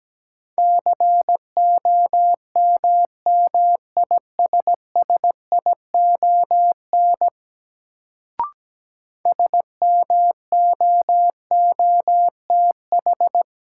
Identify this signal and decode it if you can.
Morse code, heard as a keyed tone, about 17 wpm, 700 Hz. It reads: COMMISSION SMOOTH
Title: Sentences from Top 1400 Words - New Word Drill Alt - 17wpm